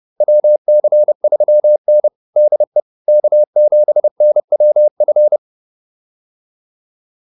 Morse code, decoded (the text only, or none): WC3N DE K7NWF